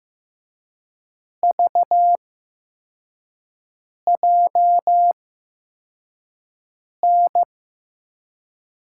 Morse code, heard as a keyed tone, about 15 wpm, spaced as for 4 wpm, 700 Hz. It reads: VJN
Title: USA Call Signs - 6 Characters - 8x Character Spacing - 15wpm